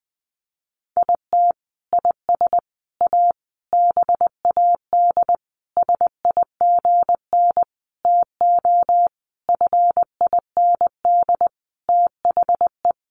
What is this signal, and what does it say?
Morse code, heard as a keyed tone, about 20 wpm, 700 Hz. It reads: IT IS A BAD SIGN TO FIND THE
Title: Sentences from Top 1000 Words 20wpm